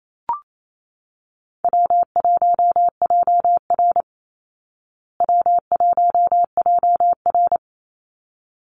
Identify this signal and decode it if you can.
Morse code, heard as a keyed tone, about 28 wpm, 700 Hz. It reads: W1JR W1JR
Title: Easy Call Signs Repeated 2x - 28wpm